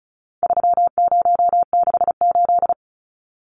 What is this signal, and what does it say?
Morse code, heard as a keyed tone, about 34 wpm, 700 Hz. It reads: 3068
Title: Numbers - 4-Digit 35wpm